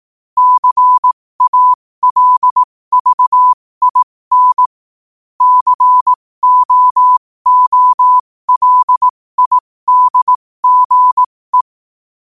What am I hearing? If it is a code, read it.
Morse code, heard as a keyed tone, about 18 wpm, spaced as for 15 wpm, 1000 Hz. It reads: CALVIN COOLIDGE
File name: qotdfiles / 15wpm